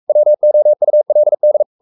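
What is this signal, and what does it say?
Morse code, heard as a keyed tone, about 44 wpm, 600 Hz. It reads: WOULD
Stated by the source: Morse code at 43 wpm sending Ken Follett's novel Never in English, from CW med Gnister.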